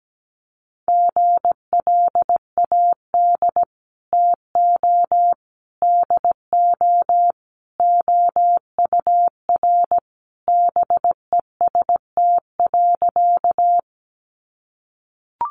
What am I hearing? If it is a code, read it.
Morse code, heard as a keyed tone, about 17 wpm, 700 Hz. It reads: GLAD TO DO OUR BEST.
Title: Sentences from Top 800 Words 17wpm